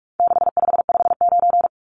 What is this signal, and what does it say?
Morse code, heard as a keyed tone, about 44 wpm, 700 Hz. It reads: 6559
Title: Numbers - 4-Digit 45wpm